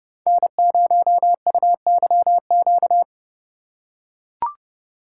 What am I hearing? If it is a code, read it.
Morse code, heard as a keyed tone, about 30 wpm, 700 Hz. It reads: N0UYQ
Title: USA Call Signs - 2x Character Spacing - 30wpm